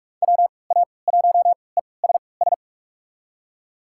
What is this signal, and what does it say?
Morse code, heard as a keyed tone, about 42 wpm, spaced as for 25 wpm, 700 Hz. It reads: WA1ESS
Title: USA Call Signs - 3x Character Spacing - 45wpm